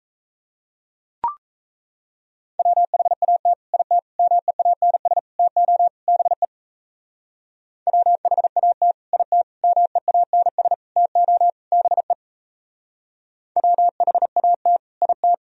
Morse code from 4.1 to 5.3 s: MEANS

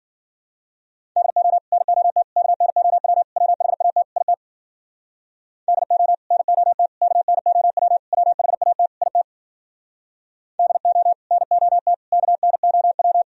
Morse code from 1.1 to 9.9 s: DO NOT KNOW WHAT IT DO NOT KNOW WHAT IT

